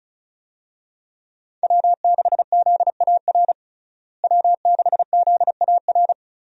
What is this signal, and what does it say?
Morse code, heard as a keyed tone, about 36 wpm, 700 Hz. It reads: W6ZAR W6ZAR
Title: USA Call Signs - 5 Characters - Repeated 2x - 35wpm